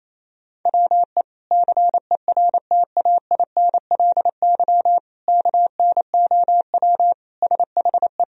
Morse code, heard as a keyed tone, about 28 wpm, 700 Hz. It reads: WE CERTAINLY KNOW SHE
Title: Sentences from Top 1000 Words - Review Part 4 - 28wpm